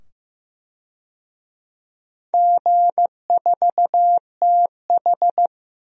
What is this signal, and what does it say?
Morse code, heard as a keyed tone, about 15 wpm, 700 Hz. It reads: G4TH
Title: DXCC-100 Call Signs - 4 Characters - 15wpm